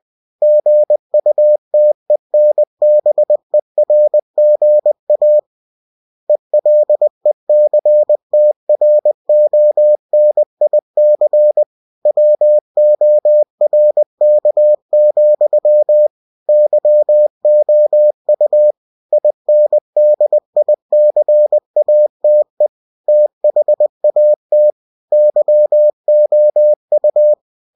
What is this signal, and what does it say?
Morse code, heard as a keyed tone, about 20 wpm, 600 Hz. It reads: GUTENBERGA ELECTRONIC WORK, YOU INDICATE THAT YOU
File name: cw-books / winnie-the-pooh / winnie-20wpm-page-0113